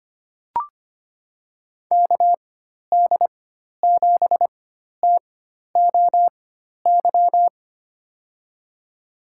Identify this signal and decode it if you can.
Morse code, heard as a keyed tone, about 25 wpm, spaced as for 12 wpm, 700 Hz. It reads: KD7TOY
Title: USA Call Signs - 6 Characters - 4x Character Spacing - 25wpm